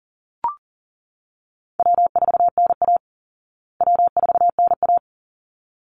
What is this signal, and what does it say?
Morse code, heard as a keyed tone, about 40 wpm, 700 Hz. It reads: W4NA W4NA
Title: Call Signs Repeated 3x - 40wpm